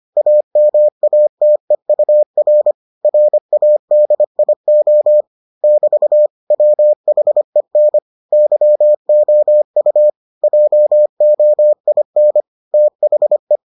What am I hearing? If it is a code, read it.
Morse code, heard as a keyed tone, about 25 wpm, 600 Hz. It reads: AMATEUR RADIO <BT> WHEN YOU JOIN THE